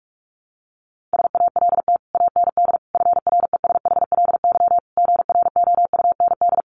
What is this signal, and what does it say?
Morse code, heard as a keyed tone, about 44 wpm, 700 Hz. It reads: SALT AND FRESHLY GROUND
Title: Sets of 5 Words Encore 45wpm